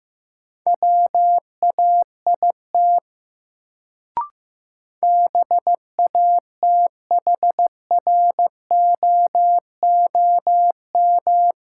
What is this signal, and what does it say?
Morse code, heard as a keyed tone, about 15 wpm, 700 Hz. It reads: WAIT BATHROOM
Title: Sets of 5 Words Encore - New Word Drill - 15wpm